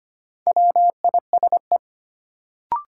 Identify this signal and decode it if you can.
Morse code, heard as a keyed tone, about 25 wpm, 700 Hz. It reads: WISE
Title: Sentences from Top 1600 Words - New Word Drill Alt - 25wpm